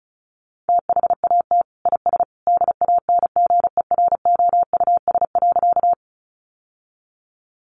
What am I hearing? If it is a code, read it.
Morse code, heard as a keyed tone, about 34 wpm, 700 Hz. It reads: THAT IS DANGEROUS.
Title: Sentences from Top 1200 Words 35wpm